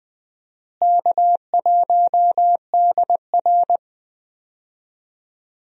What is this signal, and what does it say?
Morse code, heard as a keyed tone, about 21 wpm, 700 Hz. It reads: K1DR
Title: USA Call Signs - 4 Characters - Repeated 2x - 20wpm